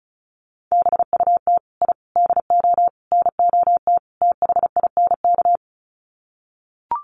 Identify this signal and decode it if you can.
Morse code, heard as a keyed tone, about 34 wpm, 700 Hz. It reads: BUT I DO NOT THINK
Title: Sets of 5 Words 35wpm